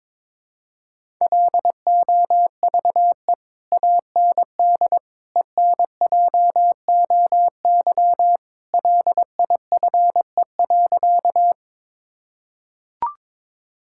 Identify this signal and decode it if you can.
Morse code, heard as a keyed tone, about 22 wpm, 700 Hz. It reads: LOVE AND ENJOY LIFE.
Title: Sentences from Top 1000 Words - Review Part 5 - 22wpm